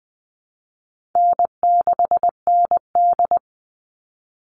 Morse code, heard as a keyed tone, about 20 wpm, 700 Hz. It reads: N6ND
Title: All Cut Numbers - 4-Digit 20wpm